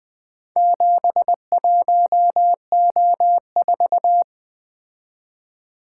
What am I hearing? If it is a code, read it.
Morse code, heard as a keyed tone, about 22 wpm, 700 Hz. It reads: 71O4
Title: Speed-Racing - Four Letter-Number Combinations - 20wpm